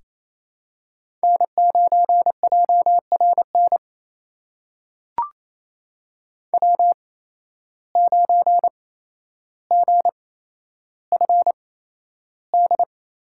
Morse code, heard as a keyed tone, about 28 wpm, spaced as for 12 wpm, 700 Hz. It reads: N9JRN W9GFD
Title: USA Call Signs - 5 Characters - 8x Character Spacing - 28wpm